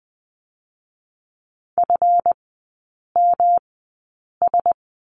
Morse code, heard as a keyed tone, about 20 wpm, spaced as for 8 wpm, 690 Hz. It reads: FMS
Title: 159 Lesson 26e - Letter g - Three Characters 20wpm